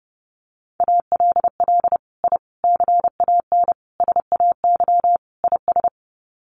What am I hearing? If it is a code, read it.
Morse code, heard as a keyed tone, about 30 wpm, 700 Hz. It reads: ALL I CAN SAY IS
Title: Sets of 5 Words Encore2 30wpm